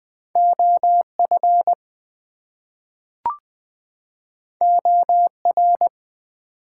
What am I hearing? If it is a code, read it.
Morse code, heard as a keyed tone, about 20 wpm, 700 Hz. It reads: OF OR
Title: Two Letter Words - 20wpm